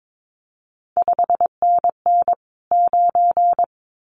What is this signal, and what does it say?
Morse code, heard as a keyed tone, about 22 wpm, 700 Hz. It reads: 5NN 9
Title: CQ WW DX Contest Exchange - 22wpm